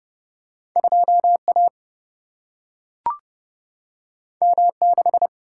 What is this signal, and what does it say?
Morse code, heard as a keyed tone, about 30 wpm, 700 Hz. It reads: 2 A M 6
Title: Single-Letter Single-Number 30wpm